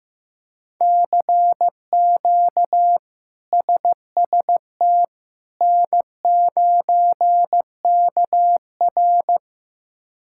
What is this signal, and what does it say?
Morse code, heard as a keyed tone, about 15 wpm, 700 Hz. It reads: CQ SST N9KR
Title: K1USN SST QSO Practice - 15wpm